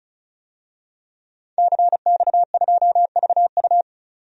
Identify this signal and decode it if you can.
Morse code, heard as a keyed tone, about 34 wpm, 700 Hz. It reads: CX2VU
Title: DXCC-100 Call Signs - 5 Characters - 35wpm